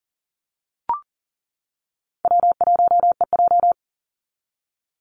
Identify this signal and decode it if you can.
Morse code, heard as a keyed tone, about 40 wpm, 700 Hz. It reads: W1EJ
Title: NA SOTA Activator Callsigns 40wpm